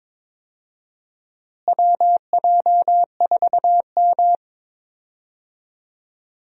Morse code, heard as a keyed tone, about 22 wpm, 700 Hz. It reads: WJ4M
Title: USA Call Signs - 4 Characters - Repeated 3x - 22wpm